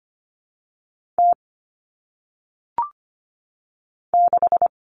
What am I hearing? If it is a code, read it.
Morse code, heard as a keyed tone, about 25 wpm, 700 Hz. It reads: T 6
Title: All Cut Numbers - 1-Digit 25wpm